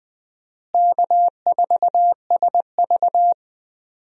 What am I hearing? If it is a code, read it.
Morse code, heard as a keyed tone, about 20 wpm, 700 Hz. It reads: K4SV
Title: USA Call Signs - 4 Characters - Repeated 2x - 20wpm